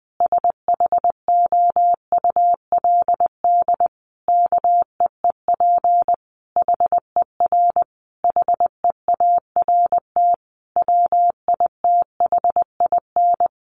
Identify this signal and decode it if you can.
Morse code, heard as a keyed tone, about 20 wpm, 700 Hz. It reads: SHOULD KEEP HER HEART WITHIN
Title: Sentences from Top 1200 Words 20wpm